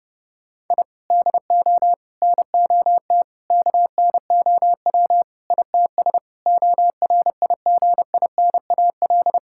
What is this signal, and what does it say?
Morse code, heard as a keyed tone, about 30 wpm, 700 Hz. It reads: I DO NOT KNOW ITS ORIGINAL
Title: Sentences from Top 1000 Words - Review Part 6 - 30wpm